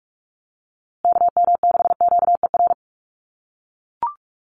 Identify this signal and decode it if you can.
Morse code, heard as a keyed tone, about 44 wpm, 700 Hz. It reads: KM6QER
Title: USA Call Signs Repeated 3x 45wpm